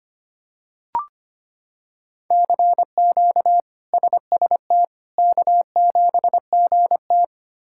Morse code, heard as a keyed tone, about 25 wpm, 700 Hz. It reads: CQ SST K7GT
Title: K1USN SST QSO Practice - 25wpm